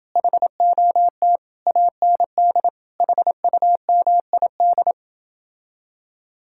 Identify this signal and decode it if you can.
Morse code, heard as a keyed tone, about 30 wpm, 700 Hz. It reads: HOT AND HUMID